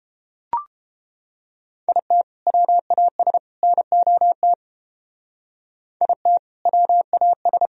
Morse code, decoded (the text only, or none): IT WAS NOT IT WAS